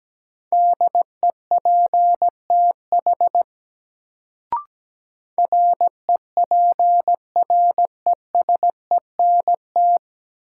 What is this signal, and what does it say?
Morse code, heard as a keyed tone, about 17 wpm, 700 Hz. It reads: DEPTH REPRESENT
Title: Sentences from Top 1200 Words - New Word Drill - 17wpm